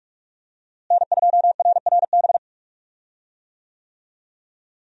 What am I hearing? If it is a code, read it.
Morse code, heard as a keyed tone, about 46 wpm, 700 Hz. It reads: N2RRB
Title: USA Call Signs - 5 Characters - Repeated 3x - 45wpm